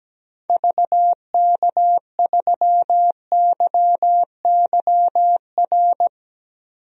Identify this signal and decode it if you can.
Morse code, heard as a keyed tone, about 17 wpm, 700 Hz. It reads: VK3YYR
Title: DXCC-100 Call Signs - 6 Characters - Repeated 2x - 17wpm